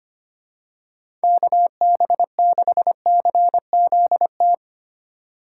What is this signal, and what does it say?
Morse code, heard as a keyed tone, about 25 wpm, 700 Hz. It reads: KB6CZT
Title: USA Call Signs - 2x Character Spacing - 25wpm